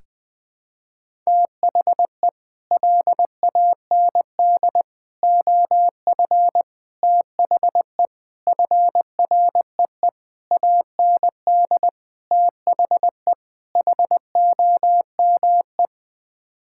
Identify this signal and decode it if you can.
Morse code, heard as a keyed tone, about 20 wpm, 700 Hz. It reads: THE LAND OF THE FREE AND THE HOME
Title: Sets of 8 Words - 20wpm